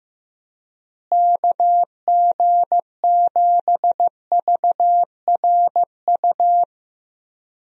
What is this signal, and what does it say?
Morse code, heard as a keyed tone, about 16 wpm, 700 Hz. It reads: KG7VRU
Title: USA Call Signs - 6 Characters - Repeated 3x - 15wpm